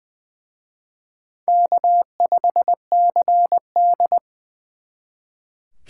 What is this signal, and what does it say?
Morse code, heard as a keyed tone, about 24 wpm, 700 Hz. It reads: K5CD
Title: Speed Racing - USA 1x2 Call Signs 20wpm